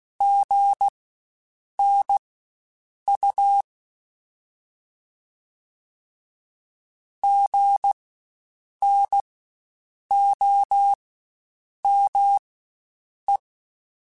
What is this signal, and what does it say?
Morse code, heard as a keyed tone, about 16 wpm, spaced as for 7 wpm, 800 Hz. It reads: GNU GNOME
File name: Spaced Alphas / Spaced Alphas mp3s / Gs sp @15